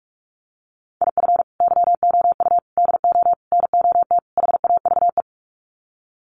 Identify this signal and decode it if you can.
Morse code, heard as a keyed tone, about 44 wpm, 700 Hz. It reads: IF YOU DO NOT HAVE
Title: Sets of 5 Words 45wpm